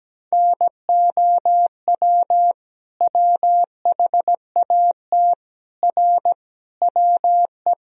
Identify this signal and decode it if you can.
Morse code, heard as a keyed tone, about 17 wpm, 700 Hz. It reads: NOW WHAT R WE
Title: Sentences from Top 100 Words and Abbreviations 17wpm